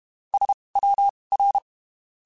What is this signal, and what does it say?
Morse code, heard as a keyed tone, about 32 wpm, spaced as for 23 wpm, 800 Hz. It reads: SWR
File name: (spaced) / QSOs Spaced-mp3s / WD0FBT 1 @30 sp